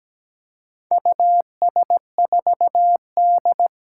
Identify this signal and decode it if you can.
Morse code, heard as a keyed tone, about 17 wpm, 700 Hz. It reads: US4D
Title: Four Challenging Letter-Number Combinations 17wpm